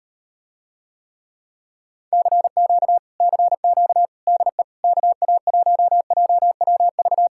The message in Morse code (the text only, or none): CQ CQ DE KA1JWV